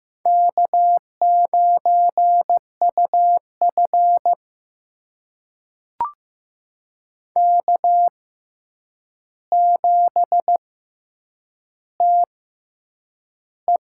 Morse code, heard as a keyed tone, about 15 wpm, spaced as for 8 wpm, 700 Hz. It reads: K9UF K7TE
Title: USA Call Signs - 4 Characters - 6x Character Spacing - 15wpm